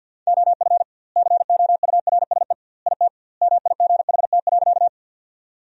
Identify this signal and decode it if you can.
Morse code, heard as a keyed tone, about 52 wpm, 700 Hz. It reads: OF COURSE IT MIGHT.
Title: 2x Word-Spacing - Sentences from Top 500 Words - 50wpm